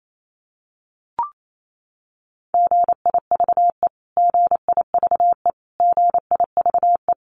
Morse code, heard as a keyed tone, about 28 wpm, 700 Hz. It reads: GIVE GIVE GIVE
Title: Top 500 Words Repeated 28wpm